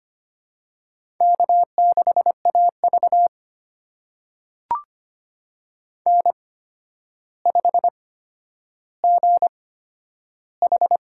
Morse code, heard as a keyed tone, about 25 wpm, spaced as for 11 wpm, 700 Hz. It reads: K6AV N5GH
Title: USA Call Signs - 4 Characters - 8x Character Spacing - 25wpm